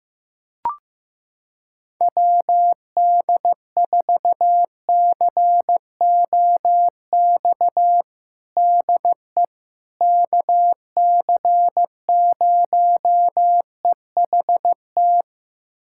Morse code, heard as a keyed tone, about 15 wpm, 700 Hz. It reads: WD4COX DE KC0EHT
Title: USA Callsign DE Callsign 15wpm